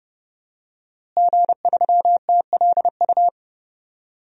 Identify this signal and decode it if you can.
Morse code, heard as a keyed tone, about 32 wpm, 700 Hz. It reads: G3TLU
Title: DXCC-100 Call Signs - 5 Characters - Repeated 3x - 30wpm